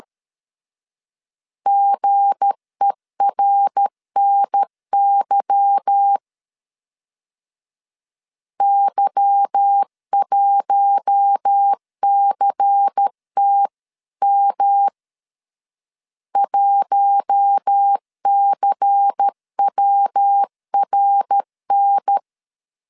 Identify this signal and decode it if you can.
Morse code, heard as a keyed tone, about 13 wpm, 800 Hz. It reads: GERNY Y1CTM 1CWRN
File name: LEZIONI / 8 YJ45 / 8 Y 12wpm 2228